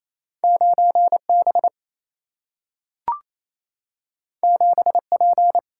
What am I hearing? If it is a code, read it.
Morse code, heard as a keyed tone, about 28 wpm, 700 Hz. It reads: 9 B 7 P
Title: Single-Letter Single-Number 28wpm